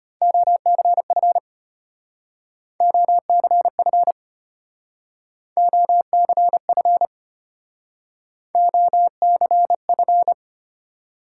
Morse code, heard as a keyed tone, about 32 wpm, 700 Hz. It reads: OCF OCF OCF OCF